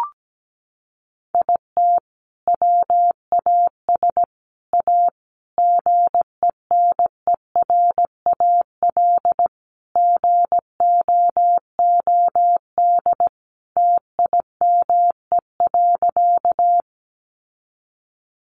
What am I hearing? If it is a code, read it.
Morse code, heard as a keyed tone, about 17 wpm, 700 Hz. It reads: IT WAS A GENERAL GOOD TIME.
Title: Sentences from Top 600 Words 17wpm